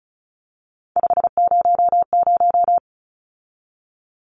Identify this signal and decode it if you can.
Morse code, heard as a keyed tone, about 34 wpm, 700 Hz. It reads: 500
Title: Numbers - 3-Digit 35wpm